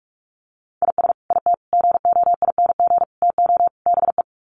Morse code, heard as a keyed tone, about 44 wpm, 700 Hz. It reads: IS IT GOING TO BE
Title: Sets of 5 Words Encore2 45wpm